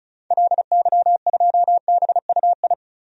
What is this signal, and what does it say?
Morse code, heard as a keyed tone, about 34 wpm, 700 Hz. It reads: LY2BUI
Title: DXCC-100 Call Signs - 6 Characters - 35wpm